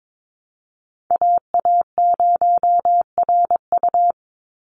Morse code, heard as a keyed tone, about 22 wpm, 700 Hz. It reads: AA0RU
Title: USA Call Signs 22wpm